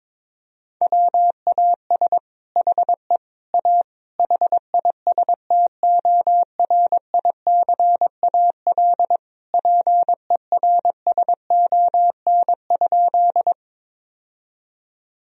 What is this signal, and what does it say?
Morse code, heard as a keyed tone, about 22 wpm, 700 Hz. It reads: WAS HE A HISTORICAL PERSON?
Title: Sentences from Top 1000 Words - Review Part 4 - 22wpm